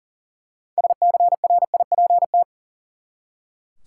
Morse code, heard as a keyed tone, about 40 wpm, 700 Hz. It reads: SCRIPT
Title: Six Letter Words - 40wpm